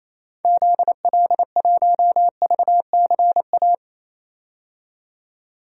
Morse code, heard as a keyed tone, about 28 wpm, 700 Hz. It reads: ZL1VCA
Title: DXCC-100 Call Signs - 6 Characters - 28wpm